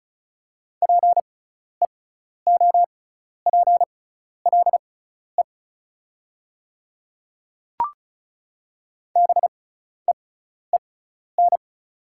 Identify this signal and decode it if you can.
Morse code, heard as a keyed tone, about 36 wpm, spaced as for 12 wpm, 700 Hz. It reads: PEOPLE BEEN
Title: 6x Character Spacing - Top 100 Words - 35wpm